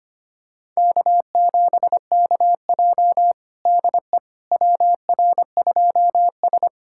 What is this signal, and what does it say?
Morse code, heard as a keyed tone, about 25 wpm, 700 Hz. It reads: K7KJ DE WR2S
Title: USA Callsign DE Callsign - 4 Characters - 25wpm